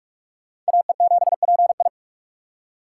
Morse code, heard as a keyed tone, about 62 wpm, spaced as for 53 wpm, 700 Hz. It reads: AE7PI
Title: Speed Racing - USA Call Signs 45wpm